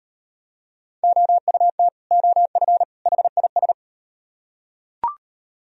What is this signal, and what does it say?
Morse code, heard as a keyed tone, about 36 wpm, 700 Hz. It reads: OUT OF HIS